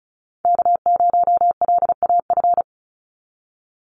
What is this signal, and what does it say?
Morse code, heard as a keyed tone, about 34 wpm, 700 Hz. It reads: K0LAF
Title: NA SOTA Chaser Callsigns 35wpm